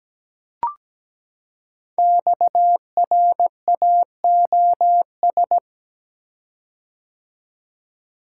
Morse code, heard as a keyed tone, about 17 wpm, 700 Hz. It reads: XRAOS
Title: Code Groups - 5 Letters - 17wpm